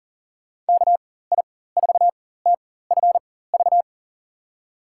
Morse code, heard as a keyed tone, about 40 wpm, spaced as for 19 wpm, 700 Hz. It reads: KI4TFV
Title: USA Call Signs - 4x Character Spacing - 40wpm